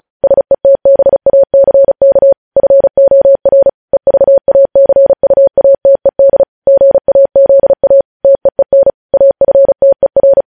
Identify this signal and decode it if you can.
Morse code, heard as a keyed tone, about 36 wpm, 550 Hz. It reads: SETBACK FOR EVACUATED GAZA TEEN AFTER